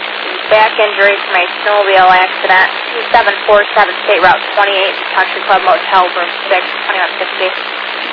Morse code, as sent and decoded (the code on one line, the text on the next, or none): none